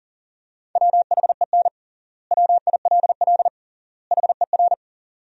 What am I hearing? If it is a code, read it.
Morse code, heard as a keyed tone, about 40 wpm, 700 Hz. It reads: WHEN WILL HER
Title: 3x Word-Spacing - Sentences from Top 200 Words - 40wpm